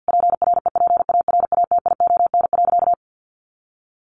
Morse code, heard as a keyed tone, about 50 wpm, 700 Hz. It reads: PREPARATION.
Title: Sentences from Top 1400 Words 50wpm